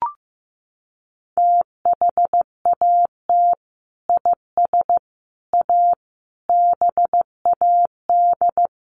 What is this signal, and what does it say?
Morse code, heard as a keyed tone, about 15 wpm, 700 Hz. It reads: THAT IS A BAD
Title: Sentences from Top 700 Words 15wpm